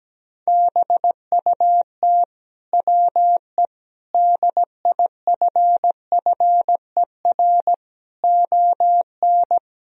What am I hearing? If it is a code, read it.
Morse code, heard as a keyed tone, about 17 wpm, 700 Hz. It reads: BUT WE DIFFER ON
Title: Sentences from Top 1000 Words - Review Part 1 - 17wpm